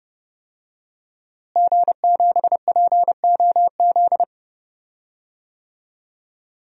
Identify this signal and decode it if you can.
Morse code, heard as a keyed tone, about 30 wpm, 700 Hz. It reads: G7POZ
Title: DXCC-100 Call Signs - 5 Characters - Repeated 3x - 30wpm